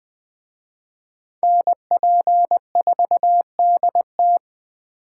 Morse code, heard as a keyed tone, about 20 wpm, 700 Hz. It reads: NP4DT
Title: Speed Racing - USA 2x2 Call Signs 17wpm